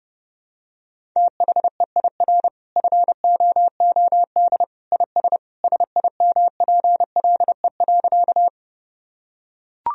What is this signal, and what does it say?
Morse code, heard as a keyed tone, about 30 wpm, 700 Hz. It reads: THEIR FOOD IS SIMPLE.
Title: Sentences from Top 1000 Words - Review Part 1 - 30wpm